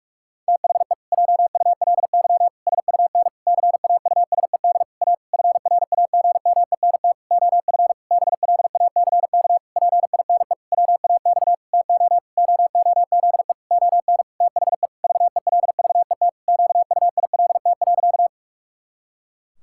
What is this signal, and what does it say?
Morse code, heard as a keyed tone, about 46 wpm, 700 Hz. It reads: THE JULY SUN CAUSED A FRAGMENT OF BLACK PINE WAX TO OOZE ON THE VELVET QUILT.
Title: Pangram Challenge v2 - 45wpm